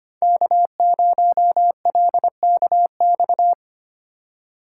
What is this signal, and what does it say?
Morse code, heard as a keyed tone, about 25 wpm, 700 Hz. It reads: K0LKX
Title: USA Call Signs - 5 Characters - 4x Character Spacing - 25wpm